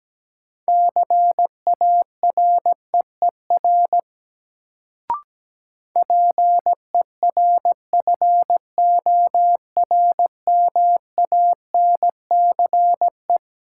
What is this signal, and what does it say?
Morse code, heard as a keyed tone, about 17 wpm, 700 Hz. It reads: CAREER PERFORMANCE
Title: Sentences from Top 1000 Words - New Word Drill - 17wpm